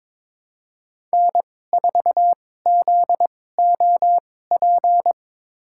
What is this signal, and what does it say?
Morse code, heard as a keyed tone, about 22 wpm, spaced as for 16 wpm, 700 Hz. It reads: N4ZOP
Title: USA Call Signs - 2x Character Spacing - 22wpm